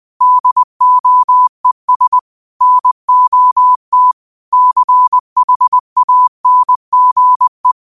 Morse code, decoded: DOES NOT CHANGE

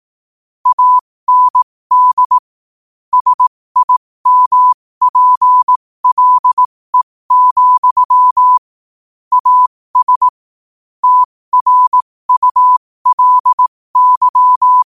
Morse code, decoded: AND SIMPLE, AS TRULY

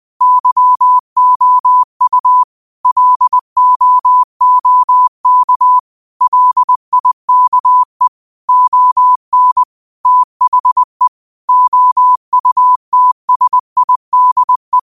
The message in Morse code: YOU LOOK LIKE ON THE OUTSIDE

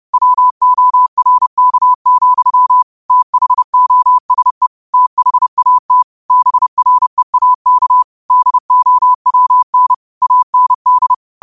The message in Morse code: WORK, THOSE THAT BREAK DOWN AND

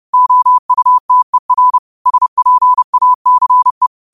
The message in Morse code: OUTER SPACE

